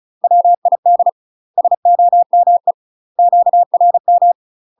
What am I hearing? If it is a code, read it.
Morse code, heard as a keyed tone, about 36 wpm, 700 Hz. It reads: WID SOME QRM
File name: UK MASTER QSOs TRAINING / 35WPM / UK MASTER QSOs 0033